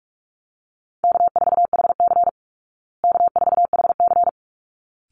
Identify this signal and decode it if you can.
Morse code, heard as a keyed tone, about 44 wpm, 700 Hz. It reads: K4HC K4HC
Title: Easy Call Signs Repeated 2x - 45wpm